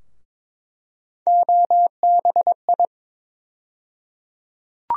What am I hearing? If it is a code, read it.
Morse code, heard as a keyed tone, about 22 wpm, 700 Hz. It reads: OBI
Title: USA Call Sign Postfixes - 3 Letters 22wpm